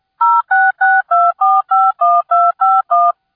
DTMF, heard as telephone touch tones, tones of about 220 ms. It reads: 0662451251